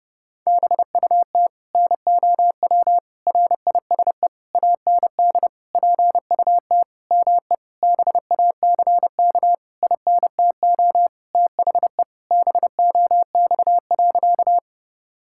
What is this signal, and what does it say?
Morse code, heard as a keyed tone, about 30 wpm, 700 Hz. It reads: BUT NOW RISE AND PUT ME BACK INTO THE BOX.
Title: Sentences from Top 1000 Words - Review Part 6 - 30wpm